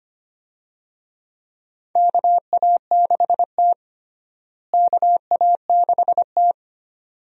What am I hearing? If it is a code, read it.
Morse code, heard as a keyed tone, about 25 wpm, 700 Hz. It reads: KA6T KA6T
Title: USA Call Signs - 4 Characters - Repeated 2x - 25wpm